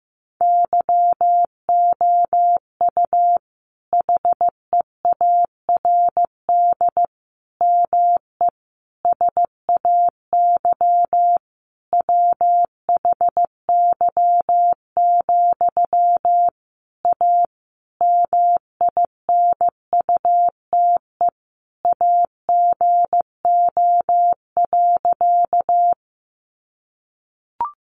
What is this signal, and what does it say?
Morse code, heard as a keyed tone, about 15 wpm, 700 Hz. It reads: YOU HEARD ME SAY WHY, A MINUTE AGO.
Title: Sentences from Top 500 Words 15wpm